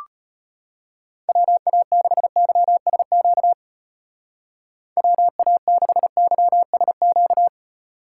Morse code, heard as a keyed tone, about 33 wpm, 700 Hz. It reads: WA6YSQ WA6YSQ